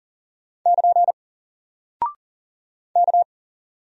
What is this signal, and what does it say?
Morse code, heard as a keyed tone, about 40 wpm, 700 Hz. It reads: <KN> K
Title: Single Character (Letter:Number:Punctuation:Prosign) Alt 40wpm